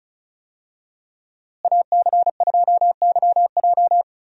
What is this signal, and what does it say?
Morse code, heard as a keyed tone, about 36 wpm, 700 Hz. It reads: AC2YJ
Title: USA Call Signs - 5 Characters - Repeated 2x - 35wpm